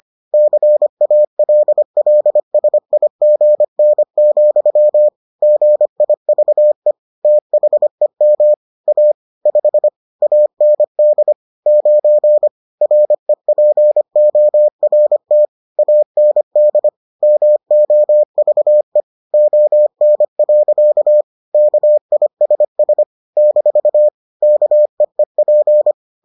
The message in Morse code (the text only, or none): CALLSIGN, GIVE THEM A 5 AND 9 REPORT AND MOVE ON. KISS - KEEP